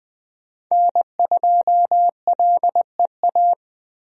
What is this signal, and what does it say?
Morse code, heard as a keyed tone, about 20 wpm, 700 Hz. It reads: N2LEA
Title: USA Call Signs - 5 Characters - Repeated 2x - 20wpm